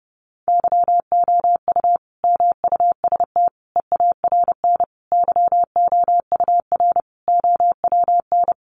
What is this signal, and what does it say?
Morse code, heard as a keyed tone, about 30 wpm, 700 Hz. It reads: YOU MUST EARN YOUR OWN
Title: Sentences from Top 1500 Words 30wpm